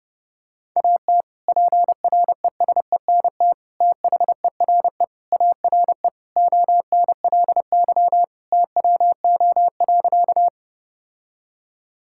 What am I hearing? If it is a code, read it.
Morse code, heard as a keyed tone, about 32 wpm, 700 Hz. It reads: AT PRESENT THERE ARE ONLY TWO.
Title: Sentences from Top 1000 Words - Review Part 5 - 30wpm